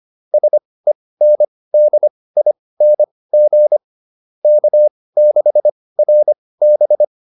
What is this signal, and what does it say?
Morse code, heard as a keyed tone, about 25 wpm, spaced as for 18 wpm, 600 Hz. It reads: SENDING K6RB